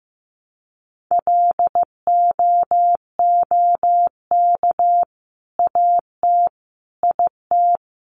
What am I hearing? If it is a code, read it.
Morse code, heard as a keyed tone, about 14 wpm, 700 Hz. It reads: LOOK AT IT
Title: Sentences from Top 200 Words 15wpm